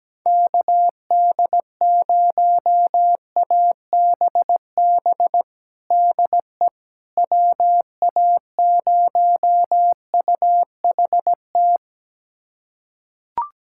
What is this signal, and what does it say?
Morse code, heard as a keyed tone, about 17 wpm, 700 Hz. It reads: KD0ABB DE WA0UHT
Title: USA Callsign DE Callsign - 6 Characters - 17wpm